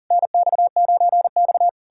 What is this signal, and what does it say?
Morse code, heard as a keyed tone, about 52 wpm, 700 Hz. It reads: NX9X